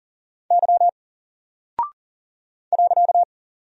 Morse code, heard as a keyed tone, about 39 wpm, 700 Hz. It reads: Y .
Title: Single Character (Letter:Number:Punctuation:Prosign) 40wpm